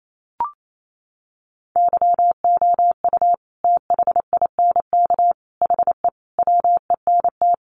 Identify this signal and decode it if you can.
Morse code, heard as a keyed tone, about 28 wpm, 700 Hz. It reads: YOU THINK HE WENT
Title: Sentences from Top 800 Words 28wpm